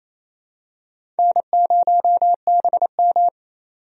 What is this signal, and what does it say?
Morse code, heard as a keyed tone, about 28 wpm, 700 Hz. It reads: N0BM
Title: USA Call Signs - 4 Characters - 28wpm